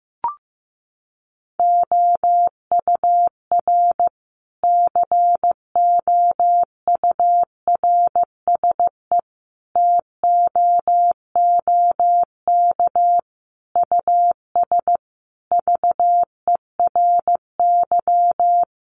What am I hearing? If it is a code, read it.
Morse code, heard as a keyed tone, about 15 wpm, 700 Hz. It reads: OUR COURSE TOOK US VERY
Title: Sentences from Top 1300 Words 15wpm